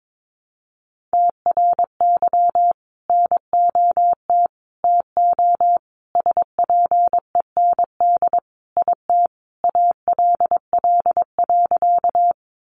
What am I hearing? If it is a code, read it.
Morse code, heard as a keyed tone, about 22 wpm, 700 Hz. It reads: TRY NOT TO SPEND IT ALL.
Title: Sentences from Top 800 Words 22wpm